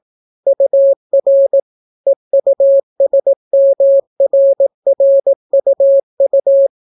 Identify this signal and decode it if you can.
Morse code, heard as a keyed tone, about 18 wpm, 550 Hz. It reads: UR EUSMRRUU